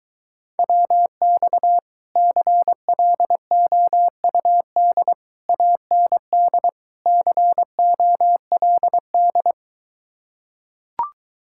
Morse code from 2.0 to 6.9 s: CLOUD AND

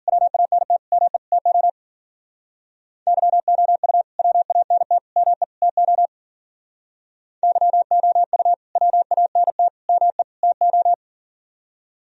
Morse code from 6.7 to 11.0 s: YOU WANT ME TO

